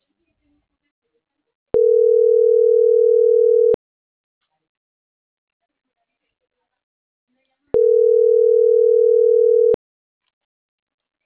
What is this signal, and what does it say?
Ringback tone (North America), 440 + 480 Hz.